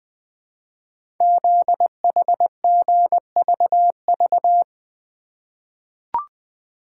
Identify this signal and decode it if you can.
Morse code, heard as a keyed tone, about 20 wpm, 700 Hz. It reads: ZHGVV
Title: Code Groups - 5 Letters - 20wpm